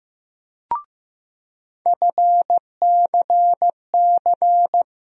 Morse code, heard as a keyed tone, about 15 wpm, 700 Hz. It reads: FCC
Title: USA Call Sign Postfixes - 3 Letters 15wpm